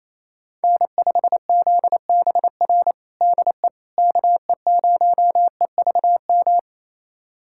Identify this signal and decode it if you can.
Morse code, heard as a keyed tone, about 28 wpm, 700 Hz. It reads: N5ZBR DE KE0EVM
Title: USA Callsign DE Callsign 28wpm